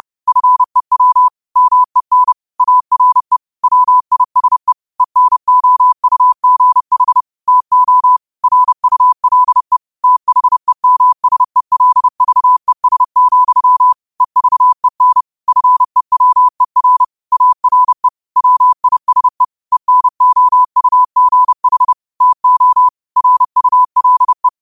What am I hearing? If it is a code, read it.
Morse code, heard as a keyed tone, about 30 wpm, 1000 Hz. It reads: FEW MEN ARE WISE ENOUGH TO RULE THEMSELVES, EVEN FEWER ARE WISE ENOUGH TO RULE
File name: qotdfiles / 30wpm